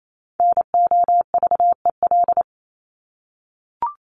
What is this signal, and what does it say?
Morse code, heard as a keyed tone, about 28 wpm, 700 Hz. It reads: NOVEL
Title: Top 1001-1250 Words 28wpm